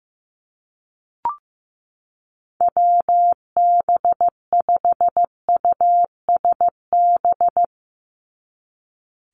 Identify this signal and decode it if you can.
Morse code, heard as a keyed tone, about 14 wpm, 700 Hz. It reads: WB5USB
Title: NA SOTA Activator Callsigns 15wpm